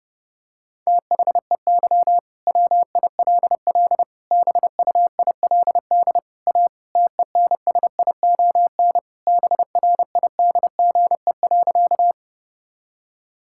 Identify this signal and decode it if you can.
Morse code, heard as a keyed tone, about 30 wpm, 700 Hz. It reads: THEY WILL BUILD A TENSION BRIDGE.
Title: Sentences from Top 1600 Words 30wpm